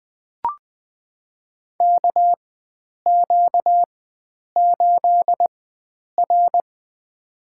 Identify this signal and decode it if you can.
Morse code, heard as a keyed tone, about 20 wpm, spaced as for 9 wpm, 700 Hz. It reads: KQ8R
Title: USA Call Signs - 4 Characters - 4x Character Spacing - 20wpm